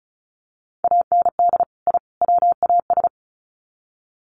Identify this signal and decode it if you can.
Morse code, heard as a keyed tone, about 34 wpm, 700 Hz. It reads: AND I WAS
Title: Sets of 3 Words 35wpm